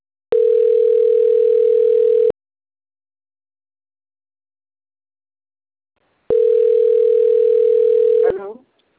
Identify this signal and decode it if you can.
Ringback tone (North America), 440 + 480 Hz, with speech.
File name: autotestagent / 11-45-51.136..callcentertransfer. .4cee7b87-9b20-4dd2-b544